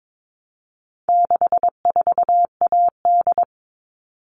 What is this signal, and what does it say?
Morse code, heard as a keyed tone, about 22 wpm, 700 Hz. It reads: 64AD
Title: All Cut Numbers - 4-Digit 22wpm